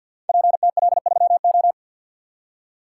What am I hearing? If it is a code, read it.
Morse code, heard as a keyed tone, about 50 wpm, 700 Hz. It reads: PTL3O
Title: Code Groups - 5 Letter:Numbers - 50wpm